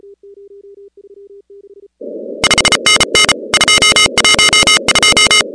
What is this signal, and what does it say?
Morse code, heard as a keyed tone, about 33 wpm, 2950 Hz, with noise. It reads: 5NN 212
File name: contests / 2019-cw-wpx-kt5j-k5pi- -2330 0-15